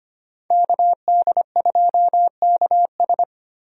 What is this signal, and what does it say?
Morse code, heard as a keyed tone, about 25 wpm, 700 Hz. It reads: KD2KS
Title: USA Call Signs - 5 Characters - Repeated 3x - 25wpm